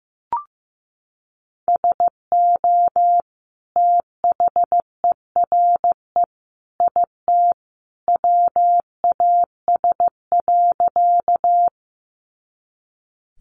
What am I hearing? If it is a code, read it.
Morse code, heard as a keyed tone, about 15 wpm, 700 Hz. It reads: SO THERE IT WAS.
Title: Sentences from Top 100 Words 15wpm